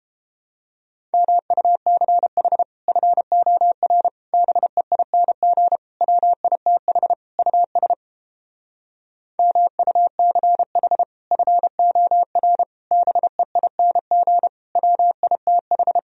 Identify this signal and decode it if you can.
Morse code, heard as a keyed tone, about 29 wpm, 700 Hz. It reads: MUCH FOR BEING WITH US MUCH FOR BEING WITH